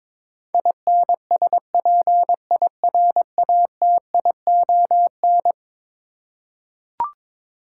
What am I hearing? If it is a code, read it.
Morse code, heard as a keyed tone, about 22 wpm, 700 Hz. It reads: INSPIRATION
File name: practice / 22wpm / Eleven Letter Words - 22wpm